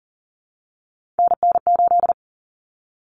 Morse code, heard as a keyed tone, about 40 wpm, 700 Hz. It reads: NN8
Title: Common Cut Numbers - 3-Digit 40wpm